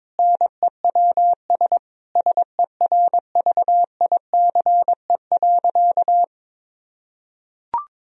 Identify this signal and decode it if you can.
Morse code, heard as a keyed tone, about 22 wpm, 700 Hz. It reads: NEWS SERVICE.
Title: Sentences from Top 1000 Words - Review Part 4 - 22wpm